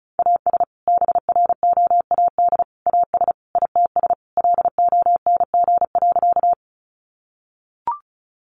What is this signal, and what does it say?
Morse code, heard as a keyed tone, about 34 wpm, 700 Hz. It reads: AS BROAD AS ITS LONG.
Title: Sentences from Top 1000 Words 35wpm